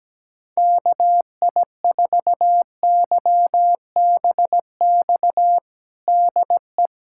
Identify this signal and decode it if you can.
Morse code, heard as a keyed tone, about 17 wpm, 700 Hz. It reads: KI4YBX DE
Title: USA Callsign DE Callsign - 6 Characters - 17wpm